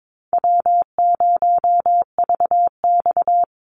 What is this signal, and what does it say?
Morse code, heard as a keyed tone, about 22 wpm, 700 Hz. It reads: W0VX
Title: Easy Call Signs Repeated 2x - 22wpm